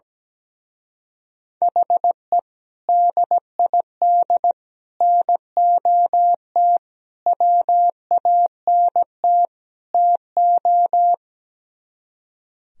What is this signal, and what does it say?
Morse code, heard as a keyed tone, about 17 wpm, 700 Hz. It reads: HE DID NOT WANT TO
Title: Speed-Racing - Sets of 5 Words 17wpm